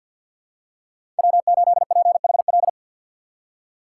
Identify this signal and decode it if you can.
Morse code, heard as a keyed tone, about 47 wpm, 700 Hz. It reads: W8PHL
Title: USA Call Signs - 5 Characters - 4x Character Spacing - 50wpm